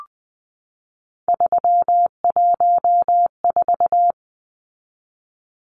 Morse code, heard as a keyed tone, about 20 wpm, 700 Hz. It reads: 314
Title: Numbers - 3-Digit 20wpm